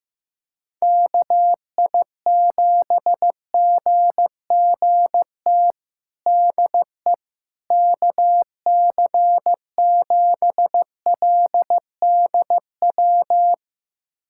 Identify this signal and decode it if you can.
Morse code, heard as a keyed tone, about 15 wpm, 700 Hz. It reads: KI7GGT DE KC7LDW
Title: USA Callsign DE Callsign - 6 Characters - 15wpm